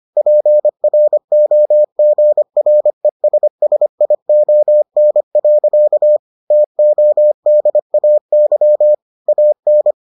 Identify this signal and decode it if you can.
Morse code, heard as a keyed tone, about 25 wpm, 600 Hz. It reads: PROGRESSION. TODAY AN